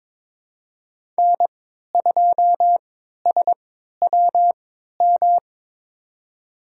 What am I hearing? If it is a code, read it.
Morse code, heard as a keyed tone, about 22 wpm, spaced as for 13 wpm, 700 Hz. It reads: N2SWM
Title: USA Call Signs - 5 Characters - 3x Character Spacing - 22wpm